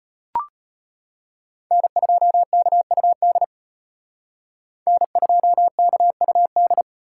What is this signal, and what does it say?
Morse code, heard as a keyed tone, about 32 wpm, 700 Hz. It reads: N2KUD N2KUD